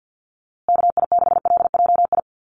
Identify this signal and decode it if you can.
Morse code, heard as a keyed tone, about 50 wpm, 700 Hz. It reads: KI6LJI
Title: USA Call Signs Repeated 2x 50wpm